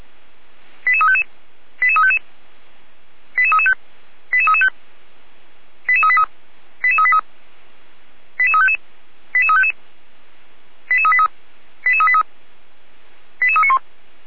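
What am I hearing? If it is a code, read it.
Five-tone selective call (ZVEI), heard as quick runs of five tones, tones of about 70 ms. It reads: 88355 88355 88335 88335 88383 88383 88355 88355 88383 88383 88391
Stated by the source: ZVEI five-tone call 88327 received by SDR.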